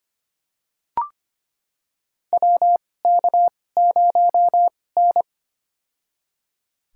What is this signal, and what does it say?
Morse code, heard as a keyed tone, about 25 wpm, spaced as for 18 wpm, 700 Hz. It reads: WK0N
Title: USA Call Signs - 4 Characters - 2x Character Spacing - 25wpm